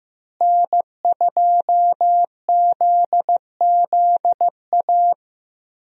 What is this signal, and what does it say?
Morse code, heard as a keyed tone, about 15 wpm, 700 Hz. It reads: N2ZZA
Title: USA Call Signs - 5 Characters - Repeated 2x - 15wpm